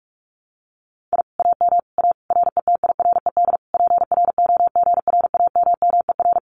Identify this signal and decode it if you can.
Morse code, heard as a keyed tone, about 46 wpm, 700 Hz. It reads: I AM A RETIRED PROGRAMMER
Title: Common QSO Phrases Encore 45wpm